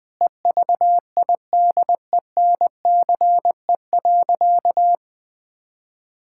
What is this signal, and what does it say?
Morse code, heard as a keyed tone, about 20 wpm, 700 Hz. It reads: EVIDENCE.
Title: Sentences with 4-Word Phrases - 20wpm